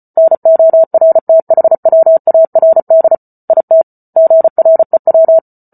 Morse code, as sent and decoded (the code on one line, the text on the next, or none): -. --- .-. - .... .-- .- .-. -.. / .. - / --. .-. .--
NORTHWARD IT GREW